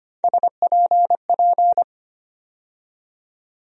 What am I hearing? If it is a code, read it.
Morse code, heard as a keyed tone, about 25 wpm, 700 Hz. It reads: SPP
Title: USA Call Sign Postfixes - 3 Letters 25wpm